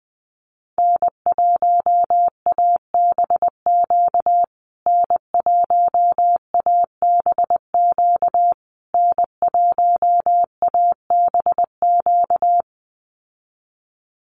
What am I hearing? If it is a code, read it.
Morse code, heard as a keyed tone, about 20 wpm, 700 Hz. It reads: N1ABQ N1ABQ N1ABQ
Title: USA Call Signs Repeated 3x 20wpm